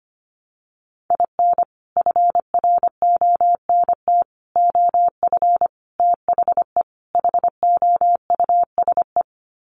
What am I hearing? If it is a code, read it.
Morse code, heard as a keyed tone, about 25 wpm, 700 Hz. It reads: IN FRONT OF THE HOUSE
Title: Sets of 5 Words Encore2 25wpm